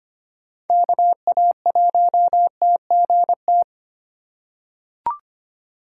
Morse code, heard as a keyed tone, about 25 wpm, 700 Hz. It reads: KA1TGT
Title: USA Call Signs - 6 Characters - 2x Character Spacing - 25wpm